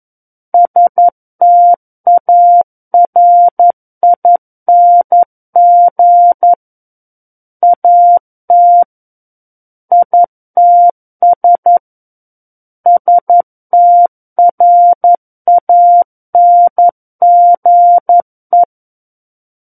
Morse code, heard as a keyed tone, about 11 wpm, 700 Hz. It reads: STARING AT ITS STRANGE
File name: 02-Chapter 2 - 11 WPM